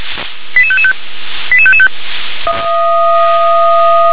Five-tone selective call (ZVEI), heard as quick runs of five tones, tones of about 70 ms. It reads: 88555 88555